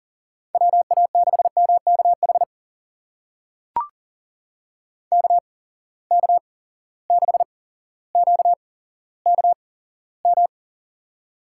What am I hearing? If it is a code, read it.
Morse code, heard as a keyed tone, about 40 wpm, spaced as for 17 wpm, 700 Hz. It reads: WA6MKH KK6QKM
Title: USA Call Signs - 6 Characters - 8x Character Spacing - 40wpm